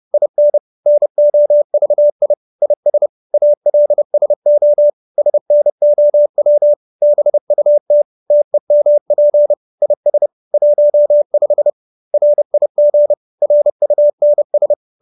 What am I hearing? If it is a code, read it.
Morse code, heard as a keyed tone, about 30 wpm, 590 Hz. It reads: IN NOVI IS ALSO SNOW BUT TEMP IS 15 RIG RUNS